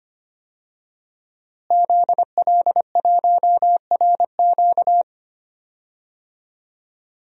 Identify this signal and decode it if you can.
Morse code, heard as a keyed tone, about 25 wpm, 700 Hz. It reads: ZL1RQ
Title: DXCC-100 Call Signs - 5 Characters - Repeated 2x - 25wpm